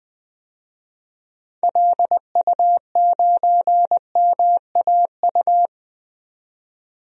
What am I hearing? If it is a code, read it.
Morse code, heard as a keyed tone, about 21 wpm, 700 Hz. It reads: LU9MAU
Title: DXCC-100 Call Signs - 6 Characters - 20wpm